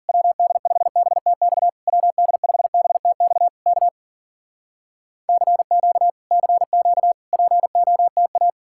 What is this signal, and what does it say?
Morse code, heard as a keyed tone, about 51 wpm, 700 Hz. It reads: WD5BTX WD5BTX K CQ CQ POTA